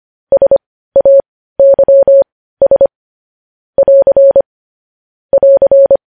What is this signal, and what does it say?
Morse code, heard as a keyed tone, about 25 wpm, spaced as for 15 wpm, 550 Hz. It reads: SAYS <AR> <AR>
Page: news headlines in morse code at 15 wpm